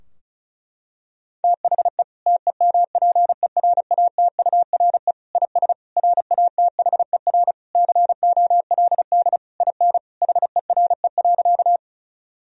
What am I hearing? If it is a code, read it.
Morse code, heard as a keyed tone, about 34 wpm, 700 Hz. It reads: THE TEMPERATURE IS RATHER COLD IN HERE.
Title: Sentences from Top 1000 Words - Review Part 4 - 35wpm